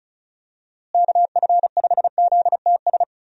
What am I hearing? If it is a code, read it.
Morse code, heard as a keyed tone, about 34 wpm, 700 Hz. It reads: KF5ZTS
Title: USA Call Signs - 6 Characters - Repeated 3x - 35wpm